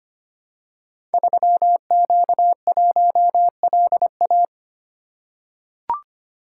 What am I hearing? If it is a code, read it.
Morse code, heard as a keyed tone, about 25 wpm, 700 Hz. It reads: 3Q1LA
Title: Code Groups - 5 Letter:Numbers - 25wpm